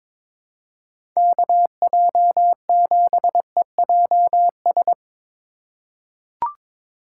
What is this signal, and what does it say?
Morse code, heard as a keyed tone, about 22 wpm, 700 Hz. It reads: KJ7EJS
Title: USA Call Signs - 6 Characters - 2x Character Spacing - 22wpm